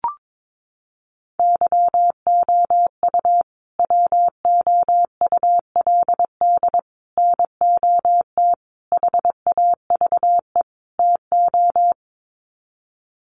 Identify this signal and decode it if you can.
Morse code, heard as a keyed tone, about 22 wpm, 700 Hz. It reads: YOU WOULD NOT HAVE TO
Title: Sets of 5 Words Encore 22wpm